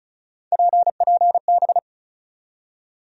Morse code, heard as a keyed tone, about 34 wpm, 700 Hz. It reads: PPB
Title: USA Call Sign Postfixes - 3 Letters 35wpm